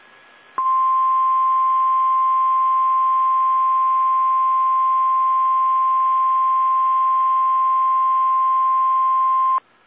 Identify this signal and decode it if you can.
Emergency Alert System (SAME) tones (the 1050 Hz attention tone). It is heard from 0.5 to 9.6 s.